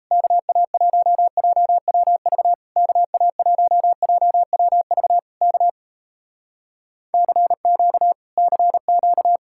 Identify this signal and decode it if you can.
Morse code, heard as a keyed tone, about 38 wpm, 700 Hz. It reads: KA1JWV KA1JWV K CQ CQ